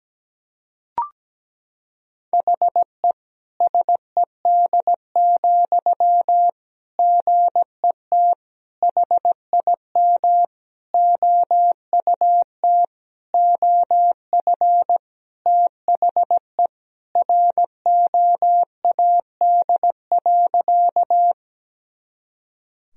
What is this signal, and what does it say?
Morse code, heard as a keyed tone, about 17 wpm, 700 Hz. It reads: HE SED, GET HIM OUT OF THE ROAD.
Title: Sentences from Top 400 Words and Abbreviations 17wpm